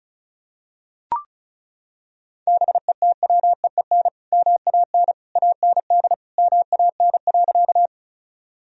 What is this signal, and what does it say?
Morse code, heard as a keyed tone, about 36 wpm, 700 Hz. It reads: BETWEEN MAN AND MAN.
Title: Sentences from Top 1000 Words - Review Part 1 - 35wpm